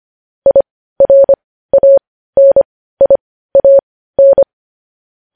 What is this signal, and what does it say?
Morse code, heard as a keyed tone, about 25 wpm, spaced as for 15 wpm, 550 Hz. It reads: IRANIAN
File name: podcast / News headlines-1x-25x15